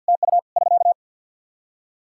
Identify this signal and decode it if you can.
Morse code, heard as a keyed tone, about 50 wpm, 700 Hz. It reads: TU <SK>
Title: K1USN SST QSO Practice 50wpm